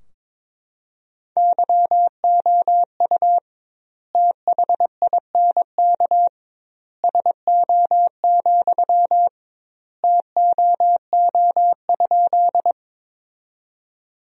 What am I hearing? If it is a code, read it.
Morse code, heard as a keyed tone, about 22 wpm, 700 Hz. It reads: YOU THINK SO, TOO?
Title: 2x Word-Spacing - Sentences from Top 200 Words - 22wpm